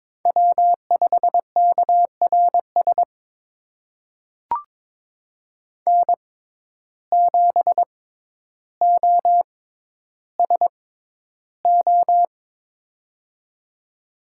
Morse code, heard as a keyed tone, about 21 wpm, spaced as for 11 wpm, 700 Hz. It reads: W5KRS N7OSO
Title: USA Call Signs - 5 Characters - 6x Character Spacing - 22wpm